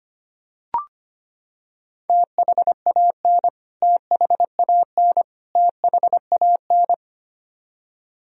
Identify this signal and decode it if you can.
Morse code, heard as a keyed tone, about 25 wpm, 700 Hz. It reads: THAN THAN THAN
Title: Top 100 Words Repeated 25wpm